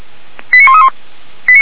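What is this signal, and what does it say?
Five-tone selective call (ZVEI), heard as quick runs of five tones, tones of about 70 ms. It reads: 89131
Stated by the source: ZVEI five-tone selective call received on SDR.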